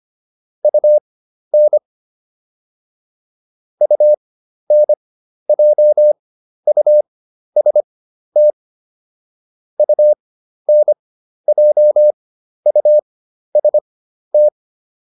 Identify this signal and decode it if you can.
Morse code, heard as a keyed tone, about 25 wpm, spaced as for 12 wpm, 600 Hz. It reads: UN UNJUST UNJUST
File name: UN6 12